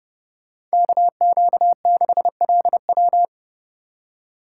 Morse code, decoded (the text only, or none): KQ6LW